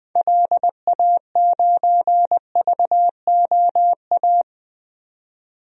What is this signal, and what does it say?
Morse code, heard as a keyed tone, about 20 wpm, 700 Hz. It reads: LA9VOA
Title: DXCC-100 Call Signs - 6 Characters - 20wpm